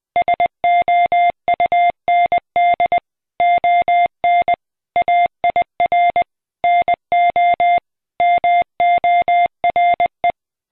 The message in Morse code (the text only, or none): SOUND ON AIR NO MORE